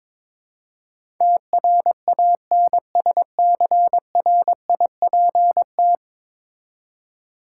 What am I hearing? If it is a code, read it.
Morse code, heard as a keyed tone, about 22 wpm, 700 Hz. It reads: TRANSCRIPT
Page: morse-code-ninja-practice-22wpm